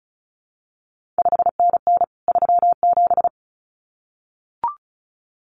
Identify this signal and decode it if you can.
Morse code, heard as a keyed tone, about 36 wpm, 700 Hz. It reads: 5NN 37
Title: CQ WW DX Contest Exchange Encore - 35wpm